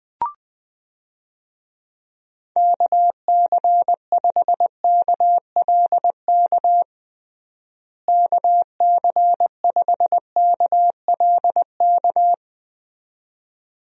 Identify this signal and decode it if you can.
Morse code, heard as a keyed tone, about 20 wpm, 700 Hz. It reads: KC5KLK KC5KLK
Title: USA Call Signs - 6 Characters - Repeated 2x - 20wpm